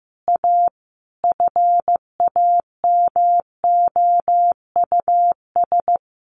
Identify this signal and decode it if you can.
Morse code, heard as a keyed tone, about 15 wpm, 700 Hz. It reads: A FAMOUS
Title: Sentences from Top 1500 Words 15wpm